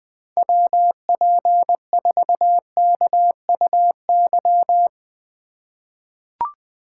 Morse code, heard as a keyed tone, about 20 wpm, 700 Hz. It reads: WP4KUY
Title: USA Call Signs - 6 Characters - 4x Character Spacing - 20wpm